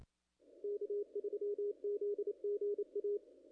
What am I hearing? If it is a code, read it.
Morse code, heard as a keyed tone, about 27 wpm, 440 Hz. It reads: K3ZGA
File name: contests / ss-cw-k5tr-op-n5rz -0100